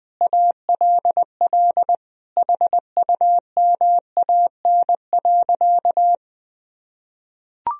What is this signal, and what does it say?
Morse code, heard as a keyed tone, about 20 wpm, 700 Hz. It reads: ALL HUMAN.
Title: Sentences from Top 1000 Words - Review Part 3 - 20wpm